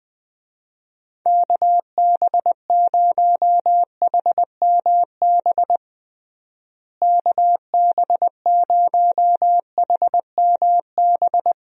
Morse code, heard as a keyed tone, about 21 wpm, 700 Hz. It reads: KB0HMB KB0HMB
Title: USA Call Signs - 6 Characters - Repeated 2x - 20wpm